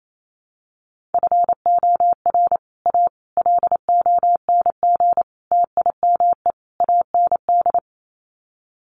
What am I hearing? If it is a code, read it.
Morse code, heard as a keyed tone, about 28 wpm, 700 Hz. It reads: FOR A LONG TIME AND
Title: Sets of 5 Words Encore2 28wpm